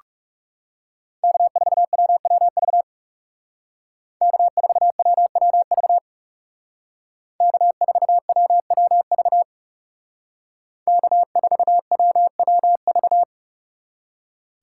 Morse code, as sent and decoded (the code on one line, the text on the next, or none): -.- ....- .-- .-- ...- / -.- ....- .-- .-- ...- / -.- ....- .-- .-- ...- / -.- ....- .-- .-- ...-
K4WWV K4WWV K4WWV K4WWV